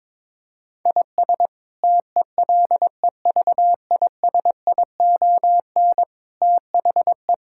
Morse code, heard as a keyed tone, about 21 wpm, 700 Hz. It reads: IS TELEVISION THE
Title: Sentences from Top 1000 Words - Review Part 6 - 22wpm